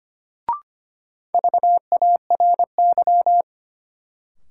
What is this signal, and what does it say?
Morse code, heard as a keyed tone, about 25 wpm, 700 Hz. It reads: VARY
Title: Sentences from Top 1200 Words - New Word Drill - 25wpm